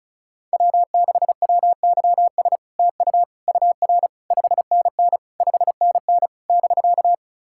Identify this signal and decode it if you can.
Morse code, heard as a keyed tone, about 38 wpm, 700 Hz. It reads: W6WYS TU UR 5NN 5NN <BK>
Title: SOTA QSO Practice 35wpm